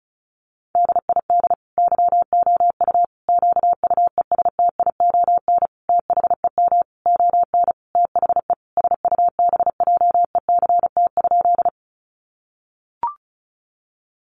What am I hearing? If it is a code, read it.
Morse code, heard as a keyed tone, about 34 wpm, 700 Hz. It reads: DID YOU QUESTION THEM ON THE SUBJECT?
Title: Sentences from Top 800 Words 35wpm